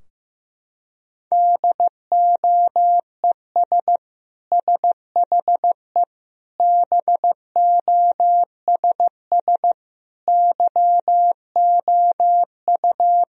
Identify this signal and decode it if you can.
Morse code, heard as a keyed tone, about 16 wpm, 700 Hz. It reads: DOES SHE BOSS YOU
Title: Sentences from Top 1000 Words - Review Part 4 - 15wpm